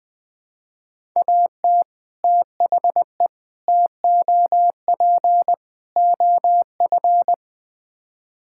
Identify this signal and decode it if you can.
Morse code, heard as a keyed tone, about 20 wpm, 700 Hz. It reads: AT THE TOP OF
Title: Build A Phrase - 4-Word Phrases - 20wpm